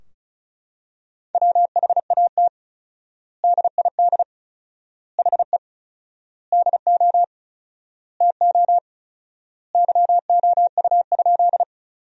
Morse code, heard as a keyed tone, about 34 wpm, 700 Hz. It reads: WHAT DID HE DO TO YOU?
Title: 4x Word-Spacing - Sentences from Top 100 Words - 35wpm